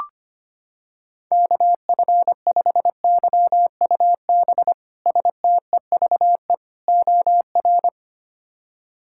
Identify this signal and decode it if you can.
Morse code, heard as a keyed tone, about 25 wpm, 700 Hz. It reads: KF5YUB STEVE OR
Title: K1USN SST QSO Practice - 25wpm